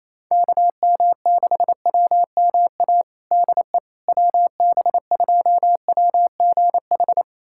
Morse code, heard as a keyed tone, about 28 wpm, 700 Hz. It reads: KM6WMA DE WB2WGH
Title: USA Callsign DE Callsign - 6 Characters - 28wpm